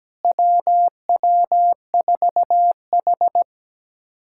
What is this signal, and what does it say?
Morse code, heard as a keyed tone, about 17 wpm, 700 Hz. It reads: WW4H
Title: USA Callsign DE Callsign - 4 Characters - 17wpm